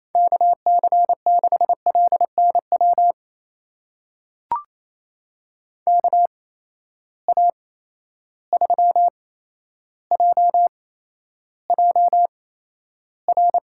Morse code, heard as a keyed tone, about 28 wpm, spaced as for 12 wpm, 700 Hz. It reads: KC6LNW KA3JJR
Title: USA Call Signs - 6 Characters - 8x Character Spacing - 28wpm